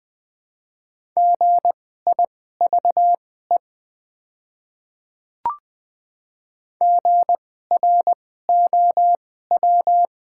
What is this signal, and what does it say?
Morse code, heard as a keyed tone, about 20 wpm, spaced as for 15 wpm, 700 Hz. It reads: GIVE GROW
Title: 2x Character Spacing - Top 300 Words - 20wpm